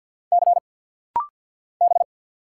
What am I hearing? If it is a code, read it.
Morse code, heard as a keyed tone, about 50 wpm, 700 Hz. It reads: C B
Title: Single Letter-Number Alt - Rapid-Fire 50wpm